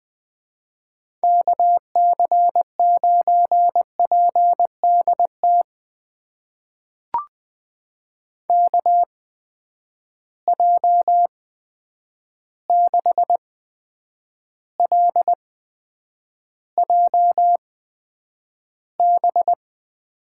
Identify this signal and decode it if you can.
Morse code, heard as a keyed tone, about 20 wpm, spaced as for 9 wpm, 700 Hz. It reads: KC9PDT KJ6LJB
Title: USA Call Signs - 6 Characters - 8x Character Spacing - 20wpm